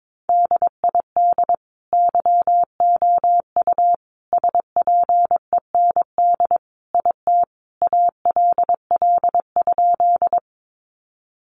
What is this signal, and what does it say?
Morse code, heard as a keyed tone, about 22 wpm, 700 Hz. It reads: DID YOU SPEND IT ALL?
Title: Sentences from Top 800 Words 22wpm